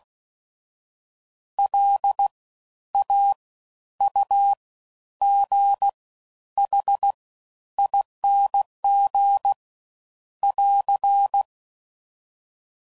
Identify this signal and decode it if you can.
Morse code, heard as a keyed tone, about 16 wpm, spaced as for 10 wpm, 800 Hz. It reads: LAUGHING <AR>
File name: Suffixes / Special INGs / Fast Ings